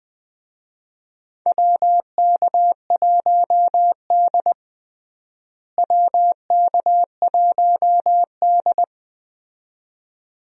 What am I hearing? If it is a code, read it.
Morse code, heard as a keyed tone, about 20 wpm, 700 Hz. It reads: WK1D WK1D
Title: USA Call Signs - 4 Characters - Repeated 2x - 20wpm